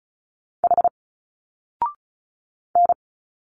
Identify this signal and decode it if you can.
Morse code, heard as a keyed tone, about 37 wpm, 700 Hz. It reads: H N
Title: Single Letters 35wpm